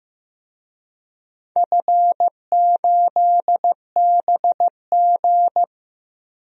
Morse code, heard as a keyed tone, about 15 wpm, 700 Hz. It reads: F8BG
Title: DXCC-100 Call Signs - 4 Characters - Repeated 2x - 15wpm